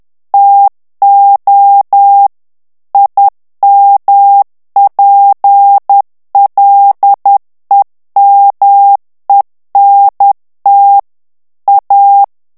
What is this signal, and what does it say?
Morse code, heard as a keyed tone, about 10 wpm, 800 Hz. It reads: TO IMPLEMENT A
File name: ANGLAIS texte clair 3 vitesse 10 mots minute